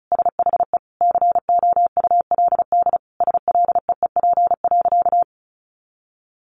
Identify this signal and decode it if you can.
Morse code, heard as a keyed tone, about 34 wpm, 700 Hz. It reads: SHE COULD SLEEP.
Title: Sentences from Top 500 Words 35wpm